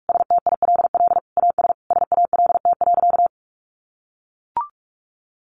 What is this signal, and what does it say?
Morse code, heard as a keyed tone, about 44 wpm, 700 Hz. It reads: STILL AS SALT.
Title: Sentences from Top 800 Words 45wpm